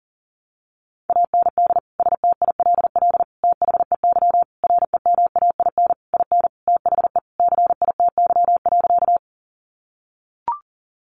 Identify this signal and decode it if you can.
Morse code, heard as a keyed tone, about 40 wpm, 700 Hz. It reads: AND STILL THEY REMAIN IN THE CITY.
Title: Sentences from Top 1200 Words 40wpm